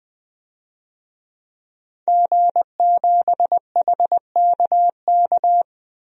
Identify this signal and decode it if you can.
Morse code, heard as a keyed tone, about 20 wpm, 700 Hz. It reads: G7HKK
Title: DXCC-100 Call Signs - 5 Characters - 20wpm